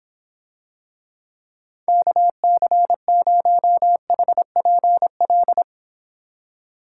Morse code, heard as a keyed tone, about 26 wpm, 700 Hz. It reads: KC0HPL